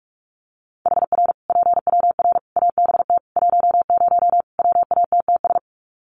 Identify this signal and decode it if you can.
Morse code, heard as a keyed tone, about 44 wpm, 700 Hz. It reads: HR PWR ABT 10 WATTS
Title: Common QSO Phrases 45wpm